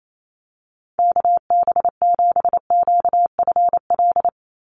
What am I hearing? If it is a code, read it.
Morse code, heard as a keyed tone, about 28 wpm, 700 Hz. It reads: KB7QFL
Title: USA Call Signs 28wpm